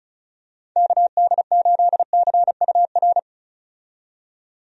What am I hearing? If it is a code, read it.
Morse code, heard as a keyed tone, about 34 wpm, 700 Hz. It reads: KD8CUR
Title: USA Call Signs - 6 Characters - 35wpm